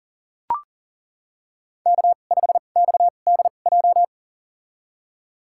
Morse code, heard as a keyed tone, about 40 wpm, spaced as for 29 wpm, 700 Hz. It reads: K5XDJ
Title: USA Call Signs - 2x Character Spacing - 40wpm